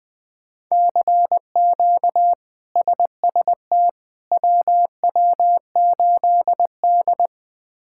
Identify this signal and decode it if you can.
Morse code, heard as a keyed tone, about 21 wpm, 700 Hz. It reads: CQ SST WW8D
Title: K1USN SST QSO Practice - 20wpm